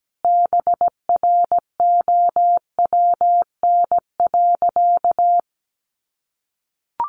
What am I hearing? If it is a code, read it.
Morse code, heard as a keyed tone, about 17 wpm, 700 Hz. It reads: BROWN.
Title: Sentences from Top 1200 Words 17wpm